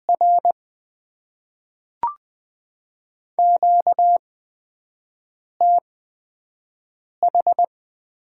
Morse code, heard as a keyed tone, about 20 wpm, 700 Hz. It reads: R Q T H
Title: Top 20 QSO Elements - 8x Character Spacing - 20wpm